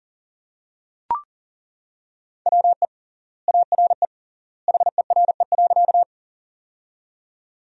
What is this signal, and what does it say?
Morse code, heard as a keyed tone, about 40 wpm, 700 Hz. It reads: WE ARE HERE.
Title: 3x Word-Spacing - Sentences from Top 200 Words - 40wpm